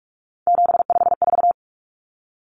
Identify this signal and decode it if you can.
Morse code, heard as a keyed tone, about 44 wpm, 700 Hz. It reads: 754
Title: Common Cut Numbers - 3-Digit 45wpm